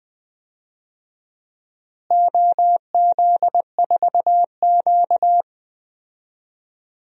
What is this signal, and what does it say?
Morse code, heard as a keyed tone, about 20 wpm, 700 Hz. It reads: OZ4Q
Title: DXCC-100 Call Signs - 4 Characters - Repeated 3x - 20wpm